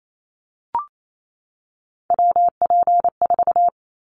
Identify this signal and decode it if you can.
Morse code, heard as a keyed tone, about 28 wpm, 700 Hz. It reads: WP4
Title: Callsign Prefixes for Top 25 DXCC - 28wpm